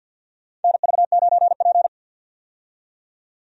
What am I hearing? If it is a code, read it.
Morse code, heard as a keyed tone, about 52 wpm, 700 Hz. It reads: NV9P
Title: USA Call Signs - 4 Characters - 50wpm